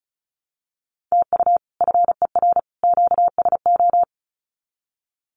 Morse code, heard as a keyed tone, about 34 wpm, 700 Hz. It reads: TU FER QSO
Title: Common QSO Phrases 35wpm